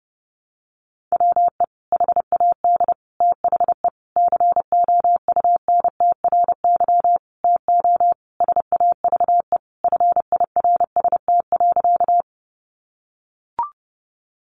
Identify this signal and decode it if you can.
Morse code, heard as a keyed tone, about 30 wpm, 700 Hz. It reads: WE HAD THE COUNTRY TO SAVE FIRST.
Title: Sentences from Top 1100 Words 30wpm